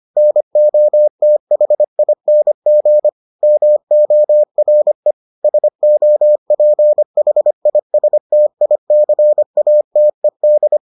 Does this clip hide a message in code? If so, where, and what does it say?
Yes: Morse code from 0.2 to 10.8 s, heard as a keyed tone, about 25 wpm, 600 Hz. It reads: NOTHING MORE SOPHISTICATED